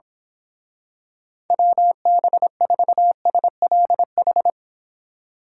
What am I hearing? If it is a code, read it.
Morse code, heard as a keyed tone, about 26 wpm, 700 Hz. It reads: WB4SLH
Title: Speed Racing - USA 2x3 Call Signs 22wpm